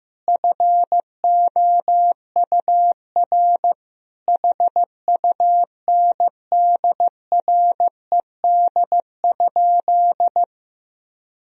Morse code, heard as a keyed tone, about 16 wpm, 700 Hz. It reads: FOUR HUNDRED?
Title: Sentences from Top 1000 Words - Review Part 2 - 15wpm